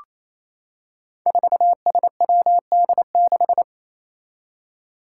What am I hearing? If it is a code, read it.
Morse code, heard as a keyed tone, about 28 wpm, 700 Hz. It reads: 4SWD6
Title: Five Challenging Letter-Number Combinations 28wpm